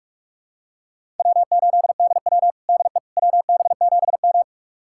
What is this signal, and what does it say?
Morse code, heard as a keyed tone, about 47 wpm, 700 Hz. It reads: W8DW DE WB7M
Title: USA Callsign DE Callsign - 4 Characters - 45wpm